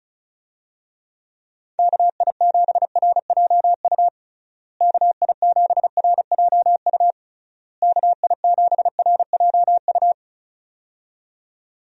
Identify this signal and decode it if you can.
Morse code, heard as a keyed tone, about 36 wpm, 700 Hz. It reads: KI7RJU KI7RJU KI7RJU
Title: USA Call Signs - 6 Characters - Repeated 3x - 35wpm